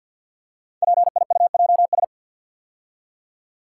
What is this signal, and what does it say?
Morse code, heard as a keyed tone, about 50 wpm, 700 Hz. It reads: PIUJS
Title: Code Groups - 5 Letters - 50wpm